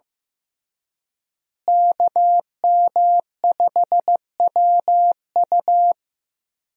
Morse code, heard as a keyed tone, about 15 wpm, 700 Hz. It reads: KM5WU
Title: USA Call Signs - 5 Characters - Repeated 2x - 15wpm